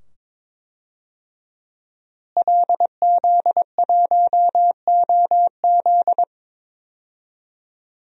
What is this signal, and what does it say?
Morse code, heard as a keyed tone, about 22 wpm, 700 Hz. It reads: LZ1OZ
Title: DXCC-100 Call Signs - 5 Characters - Repeated 2x - 22wpm